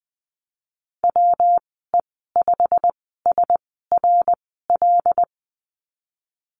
Morse code, heard as a keyed tone, about 20 wpm, spaced as for 14 wpm, 700 Hz. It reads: WE5SRL
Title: 147 Lesson 24g - Letter y - Call Signs 20wpm